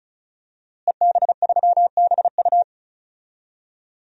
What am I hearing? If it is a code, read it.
Morse code, heard as a keyed tone, about 34 wpm, 700 Hz. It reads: EB3BU
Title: DXCC-100 Call Signs - 5 Characters - Repeated 2x - 35wpm